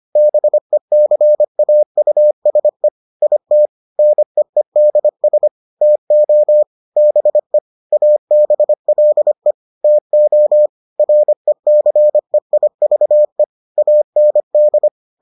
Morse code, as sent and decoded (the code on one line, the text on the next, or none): -... -.-. .- ..- ... / .. - / -. . -.. ... / - --- / -... / .- -... .-.. / - --- / .-. -.-. .. ...- . / .- -. -..
BECAUSE IT NEEDS TO BE ABLE TO RECEIVE AND